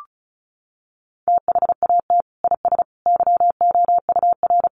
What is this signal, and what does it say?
Morse code, heard as a keyed tone, about 34 wpm, 700 Hz. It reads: THAT IS YOUR
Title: Sentences from Top 1200 Words 35wpm